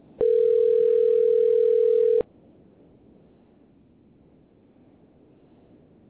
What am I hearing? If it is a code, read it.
Ringback tone (North America), 440 + 480 Hz.